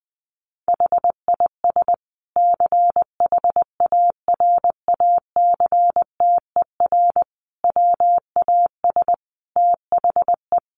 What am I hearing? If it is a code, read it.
Morse code, heard as a keyed tone, about 19 wpm, 700 Hz. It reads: HIS CHARACTER WAS THE
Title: Sentences from Top 1300 Words 20wpm